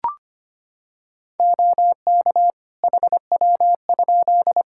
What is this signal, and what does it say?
Morse code, heard as a keyed tone, about 25 wpm, 700 Hz. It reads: OK HW?
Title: Practice QSO - Exchange 1 - Beginning - 25wpm